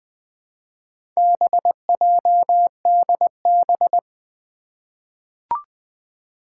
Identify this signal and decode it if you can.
Morse code, heard as a keyed tone, about 20 wpm, 700 Hz. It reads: BJDB
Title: Four Challenging Letter-Number Combinations 20wpm